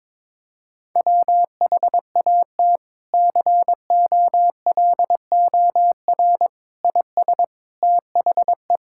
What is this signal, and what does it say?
Morse code, heard as a keyed tone, about 22 wpm, 700 Hz. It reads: WHAT COLOR IS THE
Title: Sentences from Top 1000 Words - Review Part 6 - 22wpm